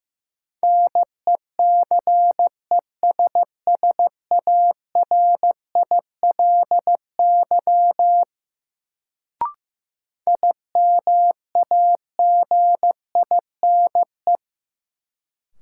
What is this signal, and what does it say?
Morse code, heard as a keyed tone, about 15 wpm, 700 Hz. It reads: NECESSARILY IMAGINE
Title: Sentences from Top 1400 Words - New Word Drill - 15wpm